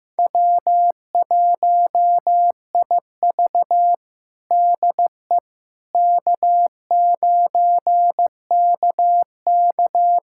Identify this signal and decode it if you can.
Morse code, heard as a keyed tone, about 15 wpm, 700 Hz. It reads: W1IV DE K9KK
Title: USA Callsign DE Callsign - 4 Characters - 15wpm